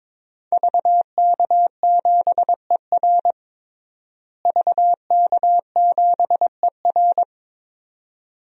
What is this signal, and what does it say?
Morse code, heard as a keyed tone, about 23 wpm, 700 Hz. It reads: VK7ER VK7ER
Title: DXCC-100 Call Signs - 5 Characters - Repeated 2x - 22wpm